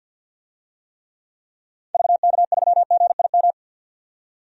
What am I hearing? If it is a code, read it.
Morse code, heard as a keyed tone, about 50 wpm, 700 Hz. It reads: VK3GIM
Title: DXCC-100 Call Signs - 6 Characters - 50wpm